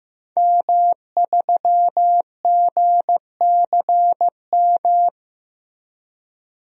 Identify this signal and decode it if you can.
Morse code, heard as a keyed tone, about 16 wpm, 700 Hz. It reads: M3GCM
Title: DXCC-100 Call Signs - 5 Characters - Repeated 2x - 15wpm